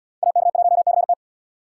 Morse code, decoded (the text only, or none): AK1CT